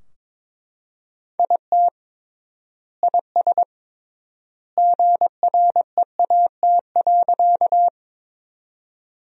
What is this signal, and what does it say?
Morse code, heard as a keyed tone, about 22 wpm, 700 Hz. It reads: IT IS GREAT.
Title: 3x Word-Spacing - Sentences from Top 200 Words - 22wpm